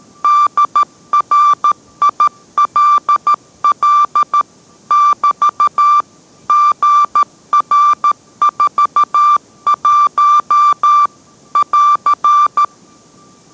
Morse code, heard as a keyed tone, about 13 wpm, 1180 Hz. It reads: DRILL <BT> GR41 <AR>